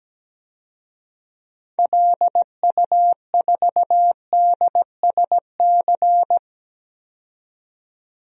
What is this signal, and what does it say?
Morse code, heard as a keyed tone, about 17 wpm, 700 Hz. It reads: LU4DSC
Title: DXCC-100 Call Signs - 6 Characters - 17wpm